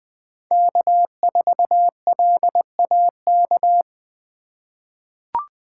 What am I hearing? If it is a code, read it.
Morse code, heard as a keyed tone, about 21 wpm, 700 Hz. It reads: K4LAK
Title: USA Call Signs - 8x Character Spacing - 20wpm